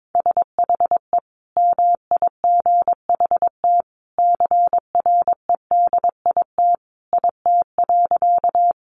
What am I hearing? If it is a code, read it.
Morse code, heard as a keyed tone, about 22 wpm, 700 Hz. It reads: SHE MIGHT CREDIT IT.
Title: Sentences from Top 1000 Words 22wpm